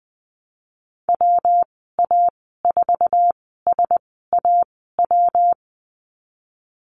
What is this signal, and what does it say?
Morse code, heard as a keyed tone, about 20 wpm, spaced as for 14 wpm, 700 Hz. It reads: WA4SAW
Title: 147 Lesson 24g - Letter y - Call Signs 20wpm